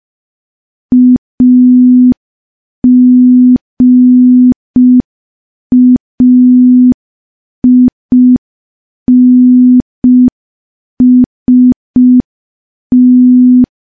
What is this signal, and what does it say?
Morse code, heard as a keyed tone, about 5 wpm, 250 Hz. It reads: AGAINST
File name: morse / morse-training-05WPM